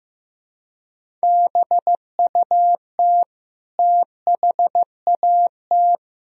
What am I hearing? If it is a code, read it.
Morse code, heard as a keyed tone, about 15 wpm, 700 Hz. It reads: BUT THAT
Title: Sentences from Top 1000 Words - Review Part 1 - 15wpm